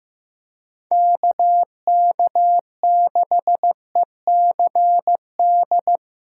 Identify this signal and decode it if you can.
Morse code, heard as a keyed tone, about 15 wpm, 700 Hz. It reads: KK6ECD
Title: USA Callsign DE Callsign - 6 Characters - 15wpm